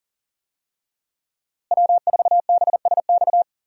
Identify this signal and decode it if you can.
Morse code, heard as a keyed tone, about 40 wpm, 700 Hz. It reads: W4BSX
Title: USA Call Signs - 5 Characters - Repeated 2x - 40wpm